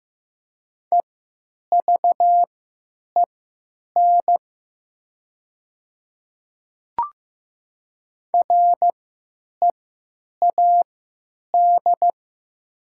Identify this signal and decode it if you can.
Morse code, heard as a keyed tone, about 15 wpm, spaced as for 9 wpm, 700 Hz. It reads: EVEN READ
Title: 3x Character Spacing - Top 200 Words - 15wpm